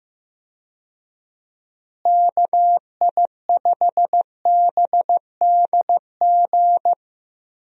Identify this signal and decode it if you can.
Morse code, heard as a keyed tone, about 15 wpm, 700 Hz. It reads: KI5BDG
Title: USA Call Signs - 6 Characters - Repeated 2x - 15wpm